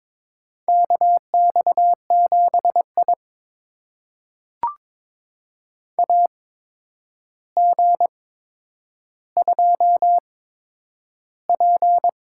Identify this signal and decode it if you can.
Morse code, heard as a keyed tone, about 23 wpm, spaced as for 10 wpm, 700 Hz. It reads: KX7I AG2P
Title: USA Call Signs - 4 Characters - 8x Character Spacing - 22wpm